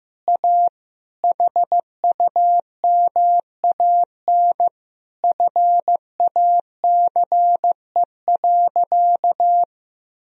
Morse code, heard as a keyed tone, about 16 wpm, 700 Hz. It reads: A HUMAN FACE.
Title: Sentences from Top 1000 Words - Review Part 3 - 15wpm